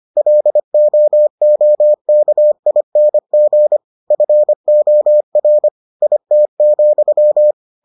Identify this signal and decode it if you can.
Morse code, heard as a keyed tone, about 25 wpm, 600 Hz. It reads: LOOKING FOR IT,